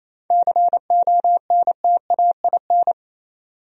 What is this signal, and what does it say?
Morse code, heard as a keyed tone, about 32 wpm, 700 Hz. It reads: CONTAIN
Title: Speed-Racing Top 500 Words 28wpm